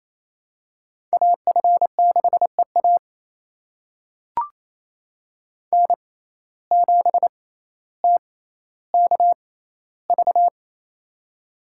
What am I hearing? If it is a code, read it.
Morse code, heard as a keyed tone, about 28 wpm, spaced as for 14 wpm, 700 Hz. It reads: AF6EA N7TKV
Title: USA Call Signs - 5 Characters - 6x Character Spacing - 28wpm